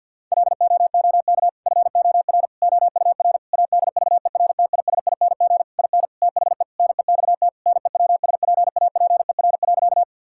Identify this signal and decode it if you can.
Morse code, heard as a keyed tone, about 50 wpm, 700 Hz. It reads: LOOK FOR OUR ADVERTISING IN THE NEXT NEWSPAPER.
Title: Sentences from Top 1000 Words - Review Part 6 - 50wpm